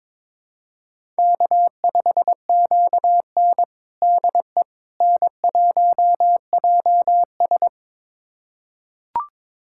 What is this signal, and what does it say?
Morse code, heard as a keyed tone, about 22 wpm, 700 Hz. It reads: K5QN DE N1JS
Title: USA Callsign DE Callsign - 4 Characters - 22wpm